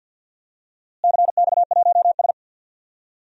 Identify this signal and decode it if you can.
Morse code, heard as a keyed tone, about 52 wpm, 700 Hz. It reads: CX1S
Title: DXCC-100 Call Signs - 4 Characters - Repeated 3x - 50wpm